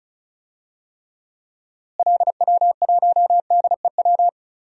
Morse code, heard as a keyed tone, about 36 wpm, 700 Hz. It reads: LW1DEW
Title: DXCC-100 Call Signs - 6 Characters - Repeated 2x - 35wpm